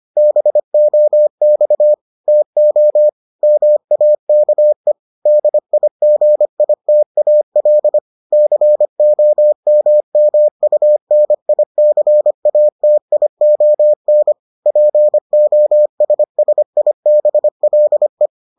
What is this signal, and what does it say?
Morse code, heard as a keyed tone, about 25 wpm, 600 Hz. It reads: BOX TO MAKE DIGITAL COMMUNICATION POSSIBLE